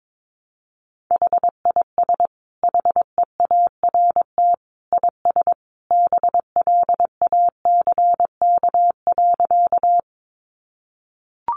Morse code, heard as a keyed tone, about 22 wpm, 700 Hz. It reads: HIS HEART IS BLACK.
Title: Sentences from Top 1200 Words 22wpm